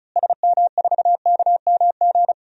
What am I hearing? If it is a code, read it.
Morse code, heard as a keyed tone, about 36 wpm, 700 Hz. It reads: SM4KMG
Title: DXCC-100 Call Signs - 6 Characters - Repeated 3x - 35wpm